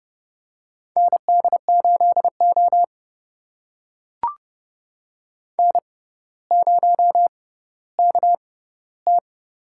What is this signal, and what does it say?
Morse code, heard as a keyed tone, about 30 wpm, spaced as for 15 wpm, 700 Hz. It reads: ND8O N0KT
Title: USA Call Signs - 4 Characters - 6x Character Spacing - 30wpm